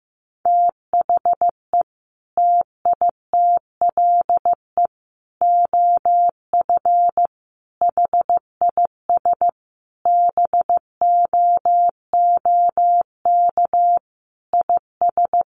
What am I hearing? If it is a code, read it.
Morse code, heard as a keyed tone, about 15 wpm, 700 Hz. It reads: THE TITLE OF HIS BOOK IS
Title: Sentences from Top 1200 Words 15wpm